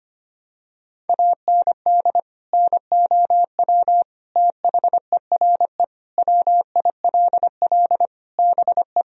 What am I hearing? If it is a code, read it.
Morse code, heard as a keyed tone, about 25 wpm, 700 Hz. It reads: AND NOW THERE WILL BE
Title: Sentences from Top 1000 Words - Review Part 2 - 25wpm